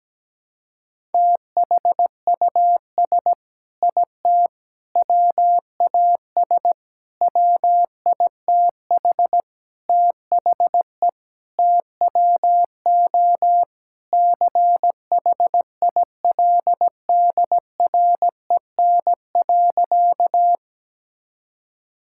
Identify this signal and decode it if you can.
Morse code, heard as a keyed tone, about 17 wpm, 700 Hz. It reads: THUS IT WAS WITH THE TWO CHILDREN.
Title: Sentences from Top 1000 Words - Review Part 5 - 17wpm